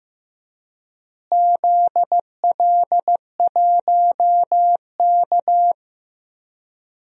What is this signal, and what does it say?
Morse code, heard as a keyed tone, about 15 wpm, 700 Hz. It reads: ZL1K
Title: DXCC-100 Call Signs - 4 Characters - Repeated 2x - 15wpm